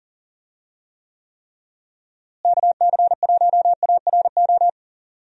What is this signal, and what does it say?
Morse code, heard as a keyed tone, about 40 wpm, 700 Hz. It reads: KC1ARO
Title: USA Call Signs - 6 Characters - Repeated 3x - 40wpm